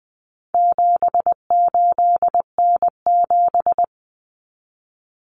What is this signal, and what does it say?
Morse code, heard as a keyed tone, about 20 wpm, 700 Hz. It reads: 78N7
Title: Common Cut Numbers - 4-Digit 20wpm